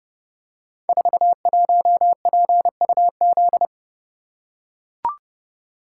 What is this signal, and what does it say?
Morse code, heard as a keyed tone, about 29 wpm, 700 Hz. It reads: 41PUZ
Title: Code Groups - 5 Letter:Numbers - 30wpm